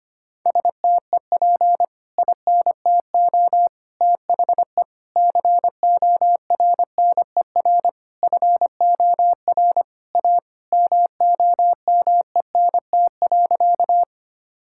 Morse code, heard as a keyed tone, about 25 wpm, 700 Hz. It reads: STEP INTO THE CORNER FOR A MOMENT.
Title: Sentences from Top 1600 Words 25wpm